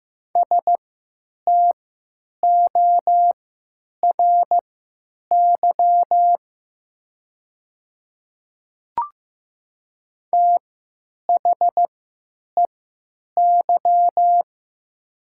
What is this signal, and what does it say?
Morse code, heard as a keyed tone, about 15 wpm, spaced as for 9 wpm, 700 Hz. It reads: STORY THEY
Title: 3x Character Spacing - Top 300 Words - 15wpm